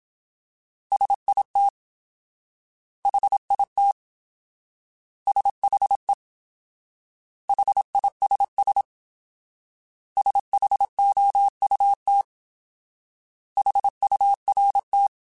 Morse code, heard as a keyed tone, about 27 wpm, 800 Hz. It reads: SIT HIT SHE HISS SHOUT HURT
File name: Alphabetical / Alphabetical mp3s / S vs H @25